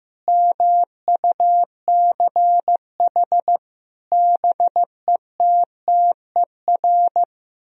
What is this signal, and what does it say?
Morse code, heard as a keyed tone, about 15 wpm, 700 Hz. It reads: MUCH BETTER
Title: Sentences from Top 1000 Words - Review Part 3 - 15wpm